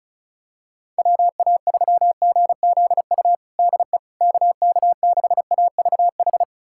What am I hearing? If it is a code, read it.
Morse code, heard as a keyed tone, about 36 wpm, 700 Hz. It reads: WA3GZU DE KK6AVH
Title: USA Callsign DE Callsign - 6 Characters - 35wpm